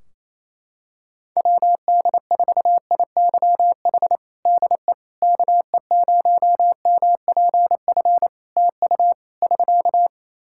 Morse code, heard as a keyed tone, about 27 wpm, 700 Hz. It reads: WD4IYH DE KE0MPF TU <SK>
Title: Practice QSO - Exchange 4 - Ending - 28wpm